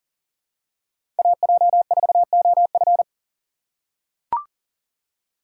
Morse code, heard as a keyed tone, about 40 wpm, 700 Hz. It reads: AJ4OF
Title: USA Call Signs - 5 Characters - 4x Character Spacing - 40wpm